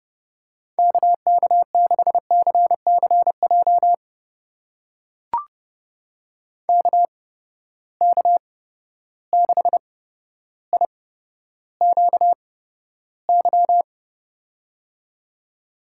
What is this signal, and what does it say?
Morse code, heard as a keyed tone, about 30 wpm, spaced as for 13 wpm, 700 Hz. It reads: KK6CCJ KK6IQY
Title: USA Call Signs - 6 Characters - 8x Character Spacing - 30wpm